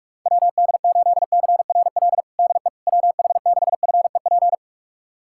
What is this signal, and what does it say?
Morse code, heard as a keyed tone, about 46 wpm, 700 Hz. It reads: WD8CRL DE WH6FEP
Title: USA Callsign DE Callsign - 6 Characters - 45wpm